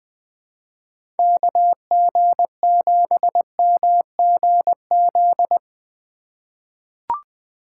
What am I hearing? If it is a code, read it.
Morse code, heard as a keyed tone, about 20 wpm, 700 Hz. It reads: KG7MGZ
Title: USA Call Signs - 6 Characters - 4x Character Spacing - 20wpm